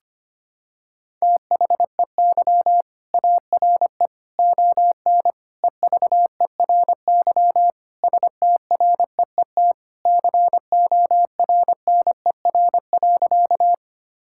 Morse code, heard as a keyed tone, about 25 wpm, 700 Hz. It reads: THEY ARE ON EVERY STREET CORNER.
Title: Sentences from Top 1600 Words 25wpm